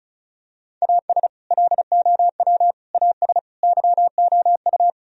Morse code, heard as a keyed tone, about 36 wpm, 700 Hz. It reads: AS LOW AS YOU
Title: Sentences from Top 1000 Words - Review Part 1 - 35wpm